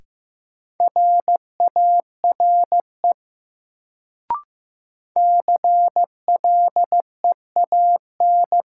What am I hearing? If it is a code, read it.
Morse code, heard as a keyed tone, about 15 wpm, 700 Hz. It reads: RARE CLEAN
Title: Sentences from Top 1100 Words - New Word Drill - 15wpm